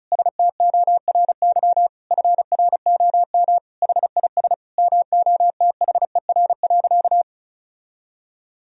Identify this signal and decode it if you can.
Morse code, heard as a keyed tone, about 36 wpm, 700 Hz. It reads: STORY FROM HIS MOTHER.
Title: Sentences from Top 1000 Words - Review Part 4 - 35wpm